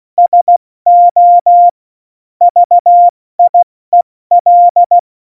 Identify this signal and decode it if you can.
Morse code, heard as a keyed tone, about 16 wpm, spaced as for 14 wpm, 700 Hz. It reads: SO VIEL
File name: Morsekurs / Witze / Witz 26. Januar 2021, 70 F80